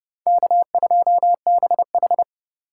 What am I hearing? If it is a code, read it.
Morse code, heard as a keyed tone, about 29 wpm, 700 Hz. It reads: K2BH